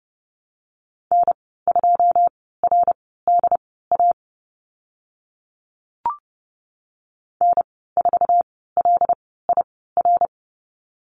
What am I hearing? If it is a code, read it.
Morse code, heard as a keyed tone, about 30 wpm, spaced as for 17 wpm, 700 Hz. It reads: N2RDA N4LIR
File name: course / 30:10wpm / 088 Lesson 15g - Number 5 - Call Signs 30wpm